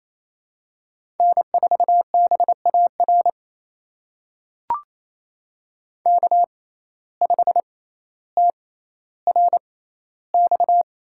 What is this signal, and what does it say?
Morse code, heard as a keyed tone, about 28 wpm, spaced as for 14 wpm, 700 Hz. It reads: N4BAR K5TRX
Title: USA Call Signs - 5 Characters - 6x Character Spacing - 28wpm